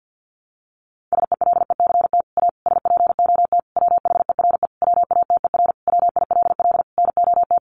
Morse code, heard as a keyed tone, about 49 wpm, 700 Hz. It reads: SELECT A SPOT WHERE WATER WILL NOT
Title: Sentences from Top 1300 Words 50wpm